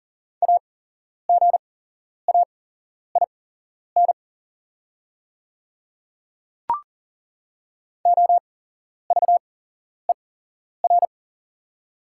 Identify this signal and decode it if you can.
Morse code, heard as a keyed tone, about 41 wpm, spaced as for 11 wpm, 700 Hz. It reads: AGAIN OVER
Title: 8x Character Spacing - Top 500 Words - 40wpm